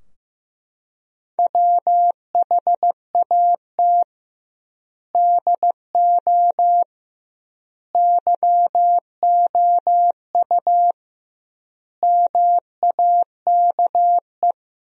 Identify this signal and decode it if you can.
Morse code, heard as a keyed tone, about 16 wpm, 700 Hz. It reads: WHAT DO YOU MAKE
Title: 2x Word-Spacing - Sentences from Top 100 Words - 15wpm